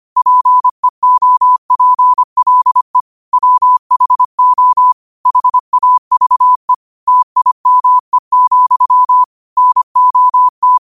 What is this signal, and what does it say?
Morse code, heard as a keyed tone, about 25 wpm, 1000 Hz. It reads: PEOPLE WHO HAVE TIME, NOT
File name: qotdfiles / 25wpm